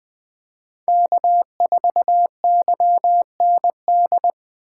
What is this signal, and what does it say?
Morse code, heard as a keyed tone, about 21 wpm, 700 Hz. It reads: K4YND
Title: K1USN SST QSO Practice - 20wpm